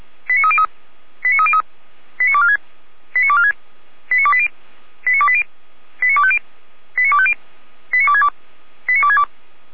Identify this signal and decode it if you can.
Five-tone selective call (ZVEI), heard as quick runs of five tones, tones of about 70 ms. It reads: 89303 89303 89246 89246 89280 89280 89250 89250 89262 89262
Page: ZVEI